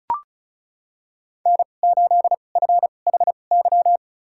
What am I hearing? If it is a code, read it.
Morse code, heard as a keyed tone, about 36 wpm, spaced as for 25 wpm, 700 Hz. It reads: N8FHY
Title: USA Call Signs - 5 Characters - 2x Character Spacing - 35wpm